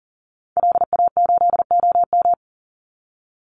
Morse code, heard as a keyed tone, about 40 wpm, 700 Hz. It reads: LA8OM
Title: Call Signs - 40wpm